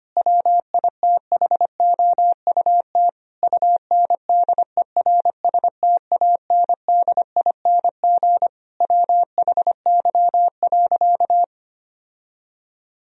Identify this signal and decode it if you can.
Morse code, heard as a keyed tone, about 25 wpm, 700 Hz. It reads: WITHOUT UNDERSTANDING WHY.
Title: Sentences from Top 1000 Words - Review Part 3 - 25wpm